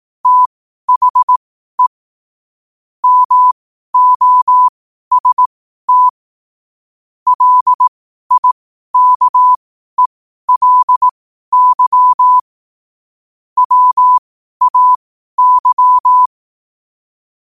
Morse code, heard as a keyed tone, about 18 wpm, spaced as for 12 wpm, 1000 Hz. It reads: THE MOST LIKELY WAY
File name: qotdfiles / 12wpm